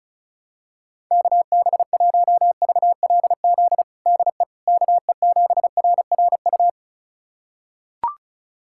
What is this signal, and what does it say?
Morse code, heard as a keyed tone, about 36 wpm, 700 Hz. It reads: KB1VLZ DE KE7RRU
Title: USA Callsign DE Callsign 35wpm